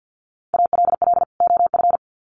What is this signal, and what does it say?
Morse code, heard as a keyed tone, about 48 wpm, 700 Hz. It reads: ALL OF
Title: Sets of 2 Words 50wpm